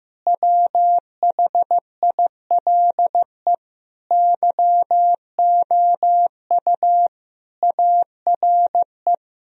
Morse code, heard as a keyed tone, about 15 wpm, 700 Hz. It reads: WHILE YOU ARE
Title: Sentences from Top 1000 Words - Review Part 1 - 15wpm